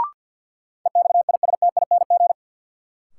Morse code, heard as a keyed tone, about 53 wpm, 700 Hz. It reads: EXISTING
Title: Sentences from Top 1500 Words - New Word Drill Alt - 50wpm